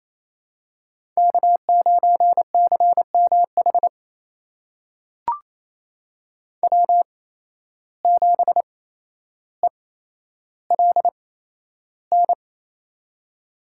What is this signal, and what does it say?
Morse code, heard as a keyed tone, about 28 wpm, spaced as for 12 wpm, 700 Hz. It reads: K9CMH W7ELN
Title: USA Call Signs - 5 Characters - 8x Character Spacing - 28wpm